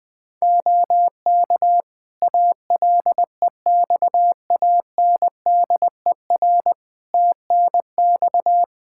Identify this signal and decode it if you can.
Morse code, heard as a keyed tone, about 20 wpm, 700 Hz. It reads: OK ALEXANDER TNX
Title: K1USN SST QSO Practice 20wpm